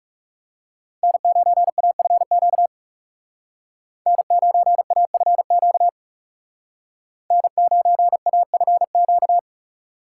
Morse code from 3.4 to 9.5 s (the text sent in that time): N9AFQ N9AFQ